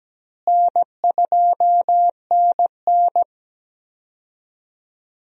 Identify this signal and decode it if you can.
Morse code, heard as a keyed tone, about 17 wpm, 700 Hz. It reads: N2NN
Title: USA Call Signs - 4 Characters - Repeated 3x - 17wpm